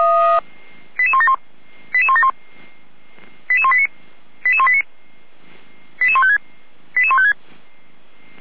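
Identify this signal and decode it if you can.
Five-tone selective call (ZVEI), heard as quick runs of five tones, tones of about 70 ms. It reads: 88171 88171 88189 88189 88156 88156
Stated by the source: Five-tone ZVEI selective call 88445 received by SDR.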